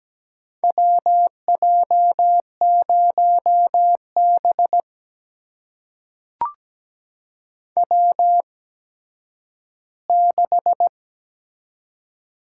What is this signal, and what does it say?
Morse code, heard as a keyed tone, about 17 wpm, spaced as for 10 wpm, 700 Hz. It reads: WJ0B W6
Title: USA Call Signs - 4 Characters - 8x Character Spacing - 17wpm